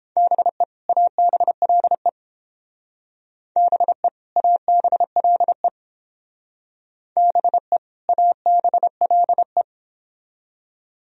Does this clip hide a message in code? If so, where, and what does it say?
Yes: Morse code from 0.2 to 9.6 s, heard as a keyed tone, about 30 wpm, 700 Hz. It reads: BE ABLE BE ABLE BE ABLE